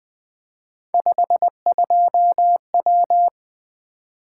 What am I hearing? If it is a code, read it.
Morse code, heard as a keyed tone, about 20 wpm, 700 Hz. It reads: 52W
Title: Three Letter-Number Combinations - 20wpm